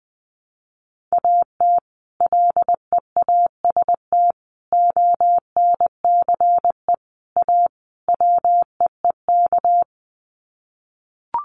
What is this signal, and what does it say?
Morse code, heard as a keyed tone, about 20 wpm, 700 Hz. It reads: AT LEAST ONCE A WEEK
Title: Sets of 5 Words Encore2 20wpm